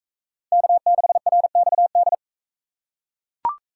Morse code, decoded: K6RXD

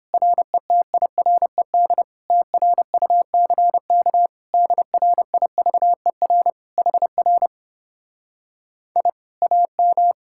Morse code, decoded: RETIRED TRUCK DRIVER HR I AM

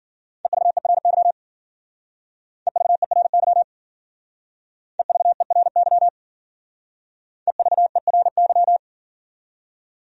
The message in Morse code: EVERY EVERY EVERY EVERY

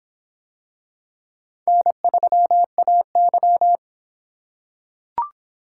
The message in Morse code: N3AY

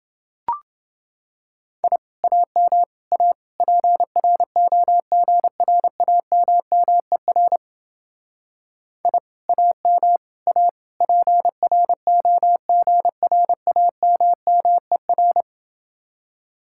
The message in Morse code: I AM A PROGRAMMER I AM A PROGRAMMER

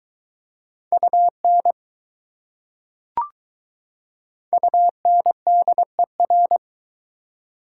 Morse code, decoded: UN UNDER

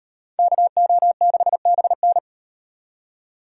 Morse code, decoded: KO6BN

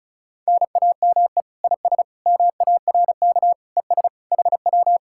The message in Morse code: NAME IS MARK ES HW

